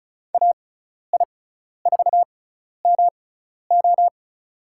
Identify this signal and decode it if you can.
Morse code, heard as a keyed tone, about 36 wpm, spaced as for 12 wpm, 700 Hz. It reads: AI4MO
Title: USA Call Signs - 5 Characters - 6x Character Spacing - 35wpm